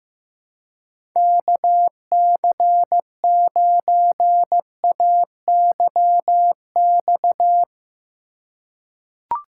Morse code, heard as a keyed tone, about 15 wpm, 700 Hz. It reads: KC9AYX
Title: USA Call Signs - 6 Characters - 3x Character Spacing - 15wpm